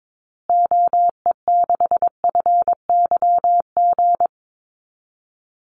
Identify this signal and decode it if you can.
Morse code, heard as a keyed tone, about 22 wpm, 700 Hz. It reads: OE6FYG
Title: Call Signs Repeated 3x - 22wpm